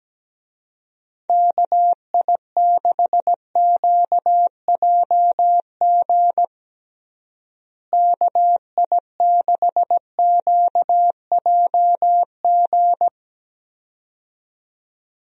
Morse code, heard as a keyed tone, about 17 wpm, 700 Hz. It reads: KI6QJG KI6QJG
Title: USA Call Signs - 6 Characters - Repeated 2x - 17wpm